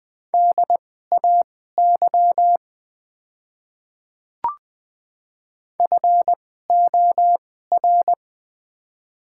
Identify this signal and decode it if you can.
Morse code, heard as a keyed tone, about 20 wpm, spaced as for 14 wpm, 700 Hz. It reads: DAY FOR
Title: 2x Character Spacing - Top 100 Words - 20wpm